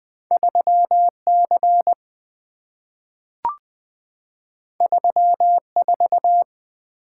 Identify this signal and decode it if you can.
Morse code, heard as a keyed tone, about 20 wpm, 700 Hz. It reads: 3 C 3 4
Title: Two Letter-Number Combinations - 20wpm